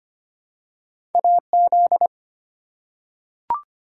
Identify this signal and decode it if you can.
Morse code, heard as a keyed tone, about 25 wpm, 700 Hz. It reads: AZ
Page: morse-code-ninja-practice-25wpm